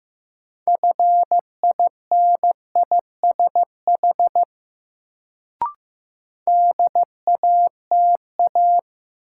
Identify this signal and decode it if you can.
Morse code, heard as a keyed tone, about 15 wpm, 700 Hz. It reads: FINISH DATA
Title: Sentences from Top 1000 Words - Review Part 3 - New Word Drill - 15wpm